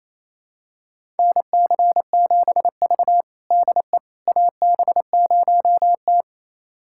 Morse code, heard as a keyed tone, about 28 wpm, 700 Hz. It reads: NC7V DE AB0T
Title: USA Callsign DE Callsign - 4 Characters - 28wpm